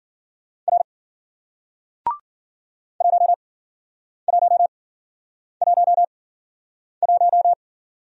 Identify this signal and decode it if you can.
Morse code, heard as a keyed tone, about 53 wpm, spaced as for 9 wpm, 700 Hz. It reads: R 1111